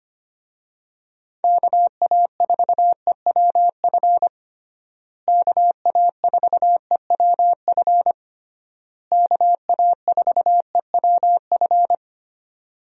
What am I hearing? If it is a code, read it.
Morse code, heard as a keyed tone, about 25 wpm, 700 Hz. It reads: KA4EWF KA4EWF KA4EWF
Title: USA Call Signs - 6 Characters - Repeated 3x - 25wpm